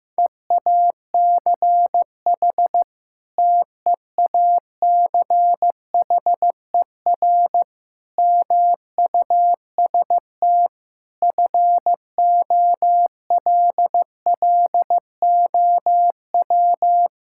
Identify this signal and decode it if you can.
Morse code, heard as a keyed tone, about 15 wpm, 700 Hz. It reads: EACH TEACHER MUST FOLLOW
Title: Sentences from Top 1000 Words - Review Part 6 - 15wpm